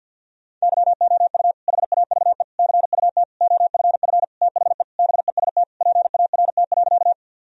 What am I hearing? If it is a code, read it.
Morse code, heard as a keyed tone, about 50 wpm, 700 Hz. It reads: YOU HAVE CUT OFF THE BEST PART.
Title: Sentences from Top 1000 Words - Review Part 1 - 50wpm